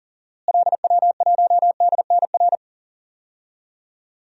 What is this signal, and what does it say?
Morse code, heard as a keyed tone, about 40 wpm, 700 Hz. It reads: LW1DNR
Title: DXCC-100 Call Signs - 6 Characters - Repeated 2x - 40wpm